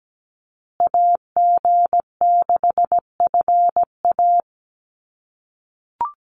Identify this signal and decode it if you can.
Morse code, heard as a keyed tone, about 17 wpm, 700 Hz. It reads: AG6FA
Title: USA Call Signs Repeated 3x 17wpm